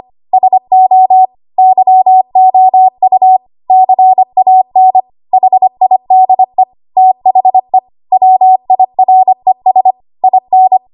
Morse code, heard as a keyed tone, about 25 wpm, 750 Hz. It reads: SO YOU CAN HIDE THE WIRES IN
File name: Archive / 25 WPM / 25WPM